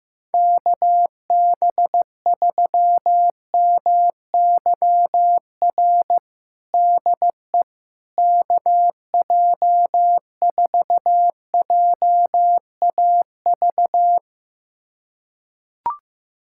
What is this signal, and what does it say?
Morse code, heard as a keyed tone, about 15 wpm, 700 Hz. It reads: KB3MYR DE KJ4JAV
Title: USA Callsign DE Callsign - 6 Characters - 15wpm